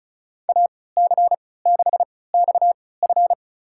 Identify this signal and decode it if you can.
Morse code, heard as a keyed tone, about 34 wpm, spaced as for 20 wpm, 700 Hz. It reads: AC6XF
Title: USA Call Signs - 5 Characters - 3x Character Spacing - 35wpm